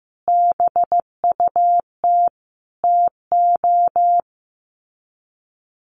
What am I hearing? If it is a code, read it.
Morse code, heard as a keyed tone, about 15 wpm, 700 Hz. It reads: BUT TO
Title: Sets of 5 Words 15wpm